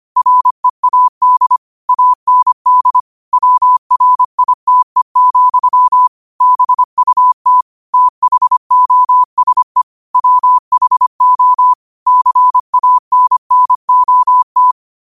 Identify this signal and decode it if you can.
Morse code, heard as a keyed tone, about 25 wpm, 1000 Hz. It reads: READ AND WRITE, BUT THOSE WHO CANNOT
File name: qotdfiles / 25wpm